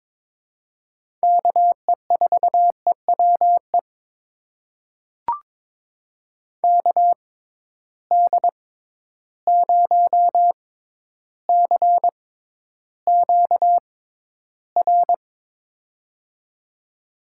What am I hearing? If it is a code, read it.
Morse code, heard as a keyed tone, about 22 wpm, spaced as for 11 wpm, 700 Hz. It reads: KE4EWE KD0CQR
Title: USA Call Signs - 6 Characters - 6x Character Spacing - 22wpm